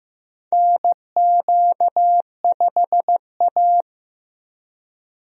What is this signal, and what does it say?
Morse code, heard as a keyed tone, about 16 wpm, 700 Hz. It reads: NQ5A
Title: USA Call Signs - 4 Characters - Repeated 2x - 15wpm